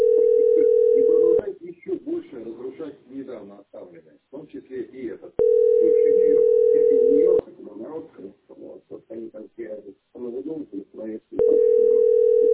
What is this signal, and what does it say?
Ringback tone (North America), 440 + 480 Hz, with speech.
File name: 956d3917-7a72-48c6-a5bb